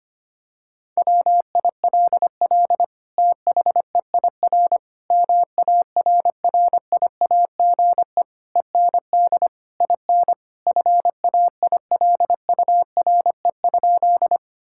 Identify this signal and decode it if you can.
Morse code, heard as a keyed tone, about 25 wpm, 700 Hz. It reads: WILL THEIR MARRIAGE END IN FAILURE?
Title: Sentences from Top 1000 Words - Review Part 6 - 25wpm